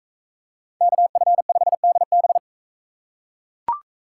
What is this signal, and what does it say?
Morse code, heard as a keyed tone, about 43 wpm, 700 Hz. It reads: KF5DB